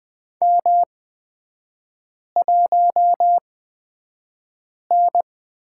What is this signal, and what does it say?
Morse code, heard as a keyed tone, about 20 wpm, spaced as for 5 wpm, 690 Hz. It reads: M1N
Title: 236 Lesson 38e - Letter q - Three Characters 20wpm